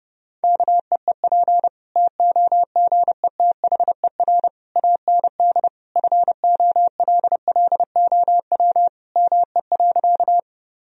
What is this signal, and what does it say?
Morse code, heard as a keyed tone, about 30 wpm, 700 Hz. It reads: KEEP TOGETHER AND FOLLOW ME.
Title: Sentences from Top 1000 Words - Review Part 1 - 30wpm